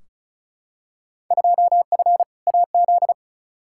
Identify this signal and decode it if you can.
Morse code, heard as a keyed tone, about 34 wpm, 700 Hz. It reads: 2F AZ
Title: Field Day Exchanges - 35wpm